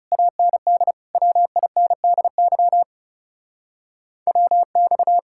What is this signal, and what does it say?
Morse code, heard as a keyed tone, about 37 wpm, 700 Hz. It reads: AND WINDY WX
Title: Speed-Racing Common QSO Phrases Encore - 30wpm